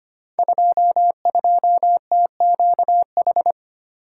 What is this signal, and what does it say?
Morse code, heard as a keyed tone, about 25 wpm, 700 Hz. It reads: 22TQH
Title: Code Groups - 5 Letter:Numbers - 25wpm